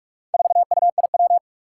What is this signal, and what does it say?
Morse code, heard as a keyed tone, about 46 wpm, 700 Hz. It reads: 4UIW
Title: Four Challenging Letter-Number Combinations 45wpm